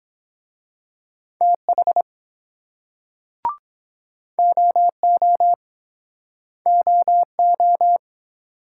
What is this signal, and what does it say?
Morse code, heard as a keyed tone, about 23 wpm, 700 Hz. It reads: TH OO OO